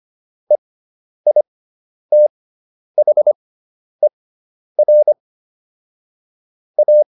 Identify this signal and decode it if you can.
Morse code, heard as a keyed tone, about 25 wpm, spaced as for 10 wpm, 600 Hz. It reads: EITHER A